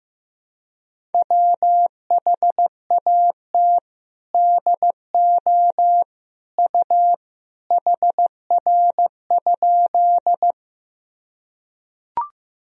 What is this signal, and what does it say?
Morse code, heard as a keyed tone, about 15 wpm, 700 Hz. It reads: WHAT DO U HR?
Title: Sentences from Top 300 Words and Abbreviations 15wpm